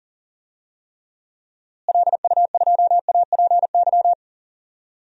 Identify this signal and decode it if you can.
Morse code, heard as a keyed tone, about 39 wpm, 700 Hz. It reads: LU2APY
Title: DXCC-100 Call Signs - 6 Characters - Repeated 2x - 40wpm